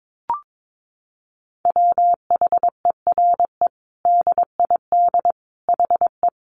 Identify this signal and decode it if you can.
Morse code, heard as a keyed tone, about 22 wpm, 700 Hz. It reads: WHERE DID HE
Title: Sentences from Top 200 Words 22wpm